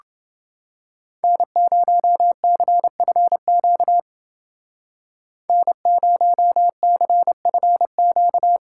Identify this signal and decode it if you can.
Morse code, heard as a keyed tone, about 29 wpm, 700 Hz. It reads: N0CFQ N0CFQ